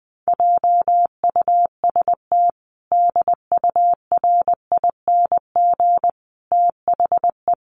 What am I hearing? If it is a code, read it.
Morse code, heard as a keyed tone, about 20 wpm, 700 Hz. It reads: JUST DURING THE
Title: Sentences from Top 700 Words 20wpm